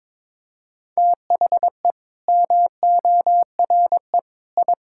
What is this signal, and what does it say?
Morse code, heard as a keyed tone, about 22 wpm, 700 Hz. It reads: THE MORE I
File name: practice / 22wpm / Sets of 8 Words - 22wpm